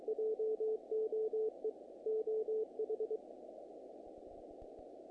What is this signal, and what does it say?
Morse code, heard as a keyed tone, about 23 wpm, 440 Hz. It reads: JOE OH
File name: contests / naqp-cw-k5tr-op-k5pi- -1900